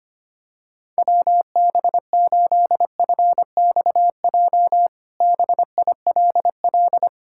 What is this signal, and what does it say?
Morse code, heard as a keyed tone, about 25 wpm, 700 Hz. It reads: WB8FXJ BILL
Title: K1USN SST QSO Practice - 25wpm